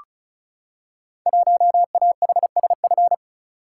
Morse code, heard as a keyed tone, about 36 wpm, 690 Hz. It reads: 1AHSF
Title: Code Groups - 5 Letter:Numbers - 35wpm